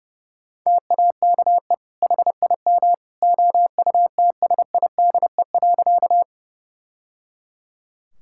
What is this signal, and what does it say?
Morse code, heard as a keyed tone, about 30 wpm, 700 Hz. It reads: TAKE HIM OUTSIDE.
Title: Sentences from Top 1000 Words - Review Part 6 - 30wpm